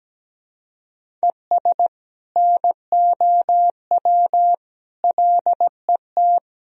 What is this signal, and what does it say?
Morse code, heard as a keyed tone, about 17 wpm, 700 Hz. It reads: ES NOW LET
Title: Sentences from Top 300 Words and Abbreviations 17wpm